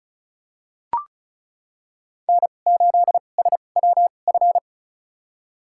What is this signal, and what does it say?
Morse code, heard as a keyed tone, about 36 wpm, spaced as for 25 wpm, 700 Hz. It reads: N8SWF
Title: USA Call Signs - 2x Character Spacing - 35wpm